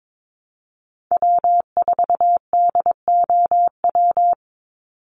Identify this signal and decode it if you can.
Morse code, heard as a keyed tone, about 22 wpm, 700 Hz. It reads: W4DOW
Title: NA SOTA Chaser Callsigns 22wpm